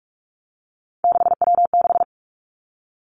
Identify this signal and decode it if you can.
Morse code, heard as a keyed tone, about 46 wpm, 700 Hz. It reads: 6W6
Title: Three Challenging Letter-Number Combinations 45wpm